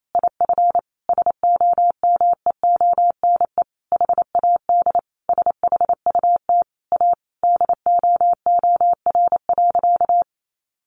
Morse code, heard as a keyed tone, about 28 wpm, 700 Hz. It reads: IF SOMEONE HAD SHUT A DOOR.
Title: A Sound Of Thunder 28wpm